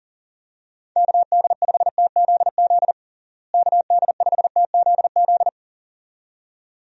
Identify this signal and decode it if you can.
Morse code, heard as a keyed tone, about 37 wpm, 700 Hz. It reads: KD5TZZ KD5TZZ
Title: USA Call Signs - 6 Characters - Repeated 2x - 40wpm